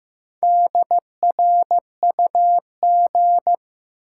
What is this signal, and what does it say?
Morse code, heard as a keyed tone, about 15 wpm, 700 Hz. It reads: DRUG
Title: Sets of 5 Words - New Word Drill - 15wpm